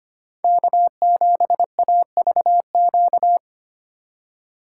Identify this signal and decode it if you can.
Morse code, heard as a keyed tone, about 25 wpm, 700 Hz. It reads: K7AVQ
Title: USA Call Signs - 5 Characters - 6x Character Spacing - 25wpm